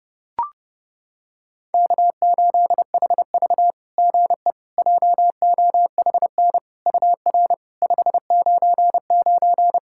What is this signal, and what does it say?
Morse code, heard as a keyed tone, about 30 wpm, 690 Hz. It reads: K8HV GE JOHN UR 599
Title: SOTA QSO Practice 30wpm